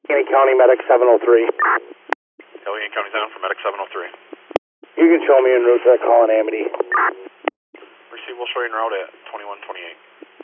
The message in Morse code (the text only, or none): none